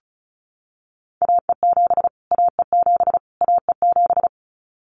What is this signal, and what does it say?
Morse code, heard as a keyed tone, about 34 wpm, 700 Hz. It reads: AE7 AE7 AE7
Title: Callsign Prefixes for Top 25 DXCC - Repeated 3x - 35wpm